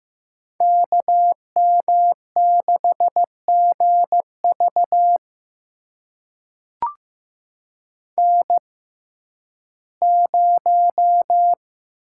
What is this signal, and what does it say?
Morse code, heard as a keyed tone, about 15 wpm, spaced as for 11 wpm, 700 Hz. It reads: KM6GV N0
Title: USA Call Signs - 5 Characters - 6x Character Spacing - 15wpm